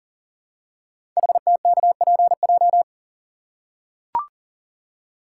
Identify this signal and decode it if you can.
Morse code, heard as a keyed tone, about 42 wpm, 700 Hz. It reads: HTKPJ
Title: Code Groups - 5 Letter:Numbers - 40wpm